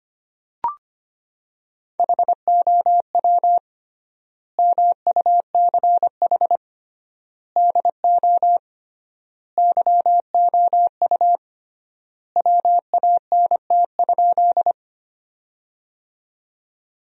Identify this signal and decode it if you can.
Morse code, heard as a keyed tone, about 25 wpm, 700 Hz. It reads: HOW MUCH DO YOU WANT?
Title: 3x Word-Spacing - Sentences from Top 200 Words - 25wpm